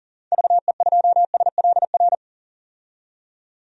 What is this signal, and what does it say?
Morse code, heard as a keyed tone, about 40 wpm, 700 Hz. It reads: VE2SLR
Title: DXCC-100 Call Signs - 6 Characters - 40wpm